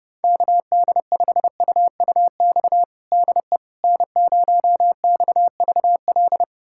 Morse code, heard as a keyed tone, about 30 wpm, 700 Hz. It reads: KD5UUX DE N0XVL
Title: USA Callsign DE Callsign 30wpm